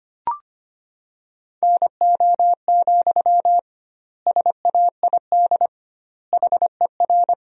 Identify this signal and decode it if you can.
Morse code, heard as a keyed tone, about 25 wpm, 700 Hz. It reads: NO, SAID HER
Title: 2x Word-Spacing - Sentences from Top 300 Words - 25wpm